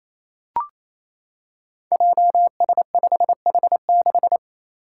Morse code, heard as a keyed tone, about 27 wpm, 700 Hz. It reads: JS5H6
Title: Five Challenging Letter-Number Combinations 28wpm